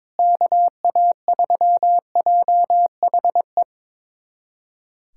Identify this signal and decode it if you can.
Morse code, heard as a keyed tone, about 22 wpm, 700 Hz. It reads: KA3JHE
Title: USA Callsign DE Callsign - 6 Characters - 22wpm